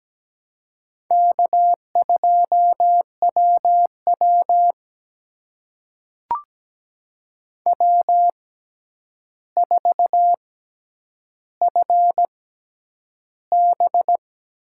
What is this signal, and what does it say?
Morse code, heard as a keyed tone, about 17 wpm, spaced as for 9 wpm, 700 Hz. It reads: K2WW W4FB
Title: USA Call Signs - 4 Characters - 6x Character Spacing - 17wpm